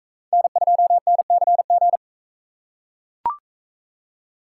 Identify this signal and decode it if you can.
Morse code, heard as a keyed tone, about 41 wpm, 700 Hz. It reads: N2NCG